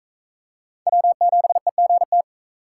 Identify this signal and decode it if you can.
Morse code, heard as a keyed tone, about 36 wpm, 700 Hz. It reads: W7EGT